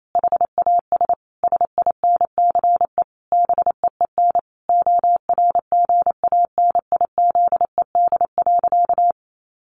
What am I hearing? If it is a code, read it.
Morse code, heard as a keyed tone, about 28 wpm, 700 Hz. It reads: HAS SINCE BEEN ORGANIZED.
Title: Sentences from Top 1400 Words 28wpm